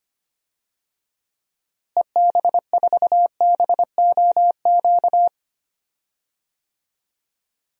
Morse code, heard as a keyed tone, about 25 wpm, 700 Hz. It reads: EB4BOQ
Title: DXCC-100 Call Signs - 6 Characters - Repeated 3x - 25wpm